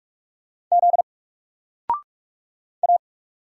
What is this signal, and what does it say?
Morse code, heard as a keyed tone, about 44 wpm, 700 Hz. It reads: Z A
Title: Single Letter-Number 45wpm